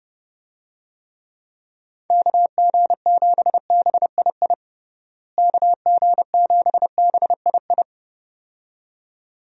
Morse code, heard as a keyed tone, about 30 wpm, 700 Hz. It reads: KG7BII KG7BII
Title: USA Call Signs - 6 Characters - Repeated 2x - 30wpm